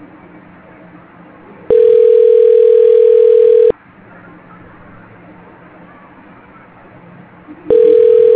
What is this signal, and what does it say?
Ringback tone (North America), 440 + 480 Hz.
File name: recordings / RE838a03e7229a3e33b3846dde37e12968